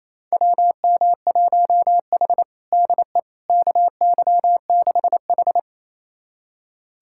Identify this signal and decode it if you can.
Morse code, heard as a keyed tone, about 28 wpm, 700 Hz. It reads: WM1H DE KY6H
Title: USA Callsign DE Callsign - 4 Characters - 28wpm